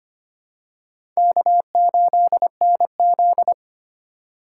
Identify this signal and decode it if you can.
Morse code, heard as a keyed tone, about 25 wpm, 700 Hz. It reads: K8NZ
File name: practice / 25wpm / Four Letter-Number Combinations - 25wpm